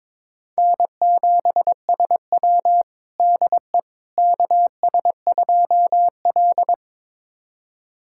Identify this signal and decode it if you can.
Morse code, heard as a keyed tone, about 22 wpm, 700 Hz. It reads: N7SW DE KS2L
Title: USA Callsign DE Callsign - 4 Characters - 22wpm